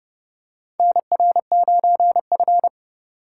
Morse code, heard as a keyed tone, about 32 wpm, 700 Hz. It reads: NR9F
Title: USA Call Signs - 4 Characters - 3x Character Spacing - 30wpm